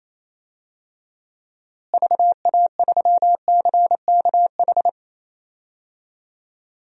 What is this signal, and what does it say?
Morse code, heard as a keyed tone, about 28 wpm, 700 Hz. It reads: VA3CKH
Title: DXCC-100 Call Signs - 6 Characters - Repeated 3x - 28wpm